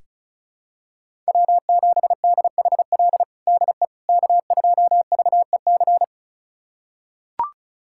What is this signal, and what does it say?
Morse code, heard as a keyed tone, about 34 wpm, 700 Hz. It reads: W7DHL DE K2VEC
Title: USA Callsign DE Callsign - 5 Characters - 35wpm